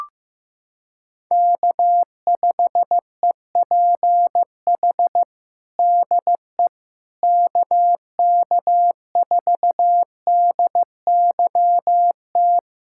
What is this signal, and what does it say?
Morse code, heard as a keyed tone, about 15 wpm, 700 Hz. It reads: K5EPH DE KK4DYT
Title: USA Callsign DE Callsign 15wpm